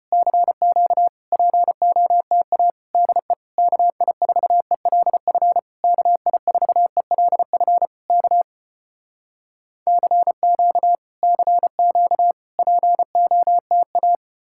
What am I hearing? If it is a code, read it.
Morse code, heard as a keyed tone, about 35 wpm, 700 Hz. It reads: CQ POTA DE KI4ELF KI4ELF K CQ CQ POTA